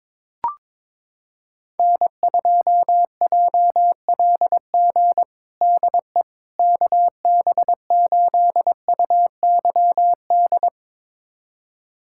Morse code, heard as a keyed tone, about 22 wpm, 700 Hz. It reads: N2JLG DE KB8UYD
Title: USA Callsign DE Callsign 22wpm